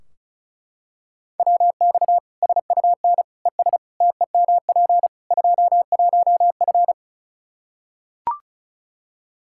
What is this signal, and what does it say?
Morse code, heard as a keyed tone, about 34 wpm, 700 Hz. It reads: WX SUN ES TEMP 21F
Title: Practice QSO - Exchange 2 - Info - 35wpm